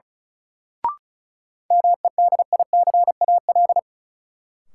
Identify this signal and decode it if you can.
Morse code, heard as a keyed tone, about 34 wpm, 700 Hz. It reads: MEDICAL
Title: Sentences from Top 800 Words - New Word Drill - 35wpm